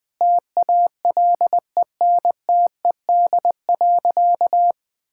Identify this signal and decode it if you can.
Morse code, heard as a keyed tone, about 21 wpm, 700 Hz. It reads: TALENTED.
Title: Sentences with 5-Word Phrases - 20wpm